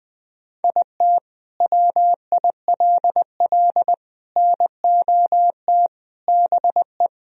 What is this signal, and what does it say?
Morse code, heard as a keyed tone, about 20 wpm, 700 Hz. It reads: IT WILL NOT BE
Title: Sentences from Top 1000 Words - Review Part 1 - 20wpm